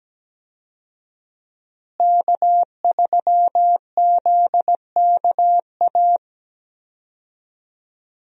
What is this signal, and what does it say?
Morse code, heard as a keyed tone, about 17 wpm, 700 Hz. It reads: K3ZKA
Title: USA Call Signs - 5 Characters - Repeated 2x - 17wpm